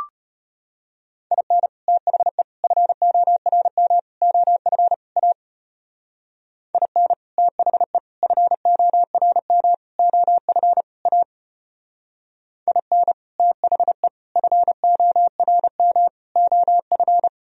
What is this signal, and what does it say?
Morse code, heard as a keyed tone, about 33 wpm, 700 Hz. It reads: IN THE FORM OF A IN THE FORM OF A IN THE FORM OF